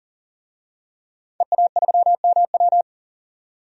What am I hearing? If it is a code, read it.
Morse code, heard as a keyed tone, about 40 wpm, 700 Hz. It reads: EA3MW
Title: DXCC-100 Call Signs - 5 Characters - 40wpm